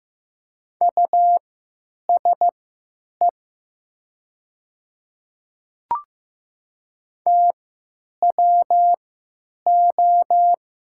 Morse code, heard as a keyed tone, about 15 wpm, spaced as for 9 wpm, 700 Hz. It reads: USE TWO
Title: 3x Character Spacing - Top 100 Words - 15wpm